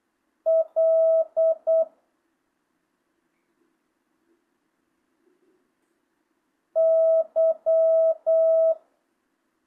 Morse code, heard as a keyed tone, about 8 wpm, 650 Hz. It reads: L Y
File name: jh8cbh / 2025cw